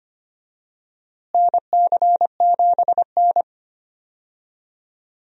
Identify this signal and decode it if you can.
Morse code, heard as a keyed tone, about 25 wpm, 700 Hz. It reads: NC7N
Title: USA Call Signs - 4 Characters - Repeated 3x - 25wpm